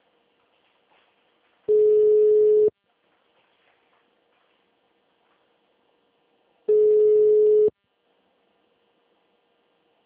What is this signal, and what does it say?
Ringback tone (Europe, 425 Hz).
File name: out- -2001- - - .16